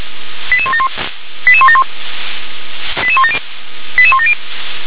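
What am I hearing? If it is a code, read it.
Five-tone selective call (ZVEI), heard as quick runs of five tones, tones of about 70 ms. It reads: 88171 88171 88170 88170
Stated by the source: Five-tone ZVEI selective call 88160 received by SDR.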